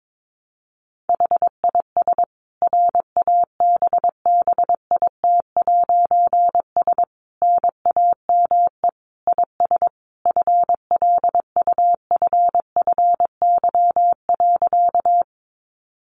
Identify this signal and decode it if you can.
Morse code, heard as a keyed tone, about 22 wpm, 700 Hz. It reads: HIS RABBIT'S NAME IS FLUFFY.
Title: Easy Phrases 22wpm